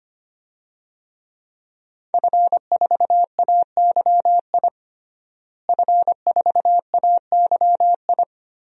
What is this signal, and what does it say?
Morse code, heard as a keyed tone, about 25 wpm, 700 Hz. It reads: F4AYI F4AYI
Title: DXCC-100 Call Signs - 5 Characters - Repeated 2x - 25wpm